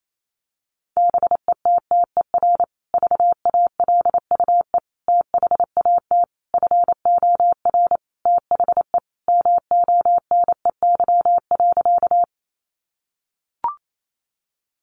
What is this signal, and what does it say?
Morse code, heard as a keyed tone, about 28 wpm, 700 Hz. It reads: BETTER VALUE THAT FOR THE MONEY.
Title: Sentences from Top 600 Words 28wpm